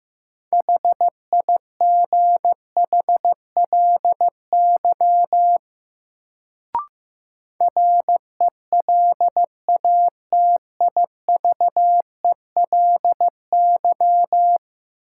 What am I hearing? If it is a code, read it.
Morse code, heard as a keyed tone, about 15 wpm, 700 Hz. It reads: HIGHLY RELATIVELY
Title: Sentences from Top 1100 Words - New Word Drill - 15wpm